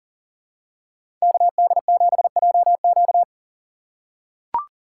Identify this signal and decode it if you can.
Morse code, heard as a keyed tone, about 40 wpm, 700 Hz. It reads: KD7JQ
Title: USA Call Signs - 3x Character Spacing - 40wpm